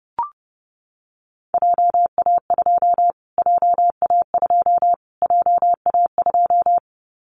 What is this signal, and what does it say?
Morse code, heard as a keyed tone, about 30 wpm, 700 Hz. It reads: JA2 JA2 JA2
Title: Callsign Prefixes for Top 25 DXCC - Repeated 3x - 30wpm